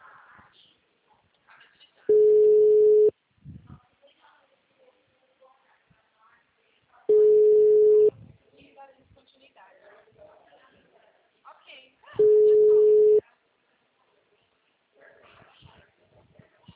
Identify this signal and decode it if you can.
Ringback tone (Europe, 425 Hz), with speech.